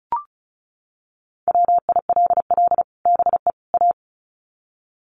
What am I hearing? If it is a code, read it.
Morse code, heard as a keyed tone, about 34 wpm, 700 Hz. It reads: WILL BE A
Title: Sets of 3 Words 35wpm